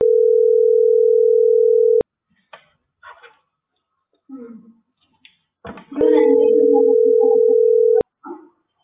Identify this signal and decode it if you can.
Ringback tone (North America), 440 + 480 Hz.